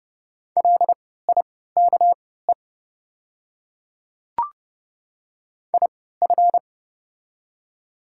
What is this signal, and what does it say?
Morse code, heard as a keyed tone, about 30 wpm, spaced as for 17 wpm, 700 Hz. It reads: LIKE IF
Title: 3x Character Spacing - Top 100 Words - 30wpm